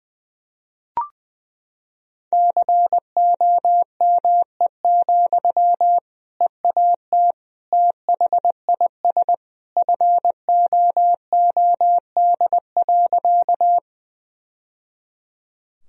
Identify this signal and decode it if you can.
Morse code, heard as a keyed tone, about 20 wpm, 700 Hz. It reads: COME, EAT THIS FOOD.
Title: Sentences from Top 1000 Words - Review Part 1 - 20wpm